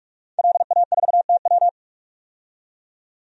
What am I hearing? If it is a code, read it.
Morse code, heard as a keyed tone, about 46 wpm, 700 Hz. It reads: LA4TW
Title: DXCC-100 Call Signs - 5 Characters - 45wpm